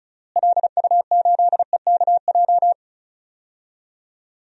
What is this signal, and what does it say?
Morse code, heard as a keyed tone, about 36 wpm, 700 Hz. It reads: LU8EKJ
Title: DXCC-100 Call Signs - 6 Characters - Repeated 3x - 35wpm